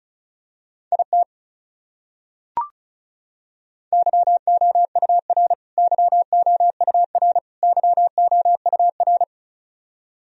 Morse code, heard as a keyed tone, about 36 wpm, 700 Hz. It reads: IT YOUR YOUR YOUR
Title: Top 100 Words Repeated 35wpm